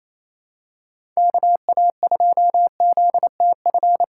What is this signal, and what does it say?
Morse code, heard as a keyed tone, about 28 wpm, 700 Hz. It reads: KA2ZTF
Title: USA Call Signs - 6 Characters - 2x Character Spacing - 28wpm